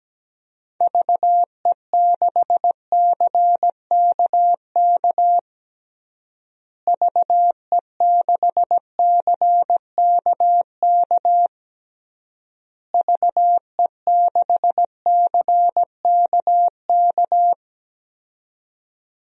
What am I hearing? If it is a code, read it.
Morse code, heard as a keyed tone, about 17 wpm, 700 Hz. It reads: VE6CKK VE6CKK VE6CKK
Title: DXCC-100 Call Signs - 6 Characters - Repeated 3x - 17wpm